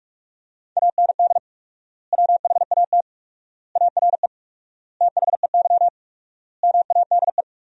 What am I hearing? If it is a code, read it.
Morse code, heard as a keyed tone, about 42 wpm, 700 Hz. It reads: AND WHAT ARE THEY MADE
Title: 4x Word-Spacing - Sentences from Top 200 Words - 45wpm